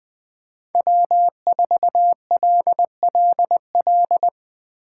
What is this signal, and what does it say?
Morse code, heard as a keyed tone, about 20 wpm, 700 Hz. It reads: W4LLL
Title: USA Call Signs - 5 Characters - 8x Character Spacing - 20wpm